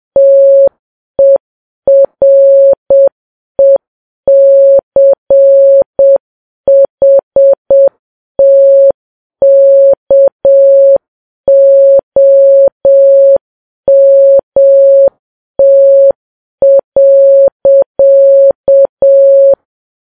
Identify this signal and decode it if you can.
Morse code, heard as a keyed tone, about 7 wpm, 550 Hz. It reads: TERECHT KOMT.
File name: Files Web tls / Morse on9fv / morse7kps